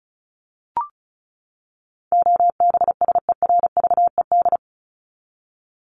Morse code, heard as a keyed tone, about 34 wpm, 700 Hz. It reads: OBSERVED
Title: ED Suffix - Top 100 Words with Base Words 35wpm